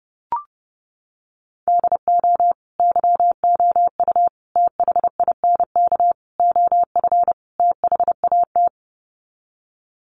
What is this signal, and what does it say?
Morse code, heard as a keyed tone, about 30 wpm, 700 Hz. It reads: DO YOU THINK OF THAT
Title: Sets of 5 Words Encore2 30wpm